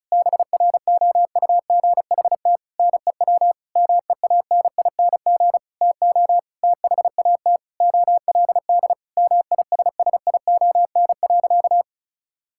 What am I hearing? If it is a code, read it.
Morse code, heard as a keyed tone, about 36 wpm, 700 Hz. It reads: BROUGHT NEW MEANING TO THAT OLD MISSION.
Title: Sentences from Top 1600 Words 35wpm